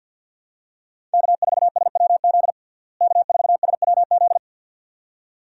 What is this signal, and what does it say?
Morse code, heard as a keyed tone, about 47 wpm, 700 Hz. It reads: K4SWZ K4SWZ
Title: USA Call Signs - 5 Characters - Repeated 2x - 50wpm